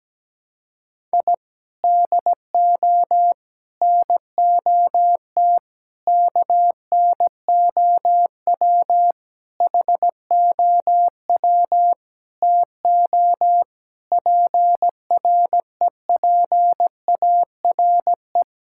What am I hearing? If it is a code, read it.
Morse code, heard as a keyed tone, about 17 wpm, 700 Hz. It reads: I DO NOT KNOW HOW TO PREPARE
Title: Sentences from Top 1000 Words - Review Part 6 - 17wpm